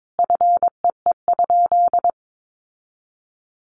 Morse code, heard as a keyed tone, about 22 wpm, 700 Hz. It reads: FEE?
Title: Sentences from Top 1500 Words 22wpm